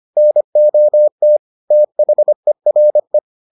Morse code, heard as a keyed tone, about 25 wpm, 600 Hz. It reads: NOT THERE